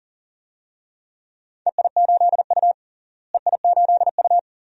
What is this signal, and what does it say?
Morse code, heard as a keyed tone, about 37 wpm, 700 Hz. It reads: EI8U EI8U
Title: DXCC-100 Call Signs - 4 Characters - Repeated 2x - 40wpm